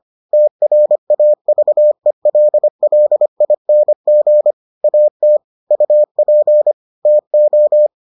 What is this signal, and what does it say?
Morse code, heard as a keyed tone, about 25 wpm, 600 Hz. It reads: TRAVELLING AT UP TO